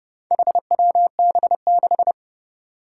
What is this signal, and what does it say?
Morse code, heard as a keyed tone, about 30 wpm, 700 Hz. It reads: HWB6
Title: Four Challenging Letter-Number Combinations 30wpm